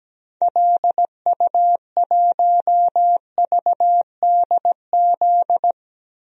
Morse code, heard as a keyed tone, about 17 wpm, 700 Hz. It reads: LU1VDZ
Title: DXCC-100 Call Signs - 6 Characters - 17wpm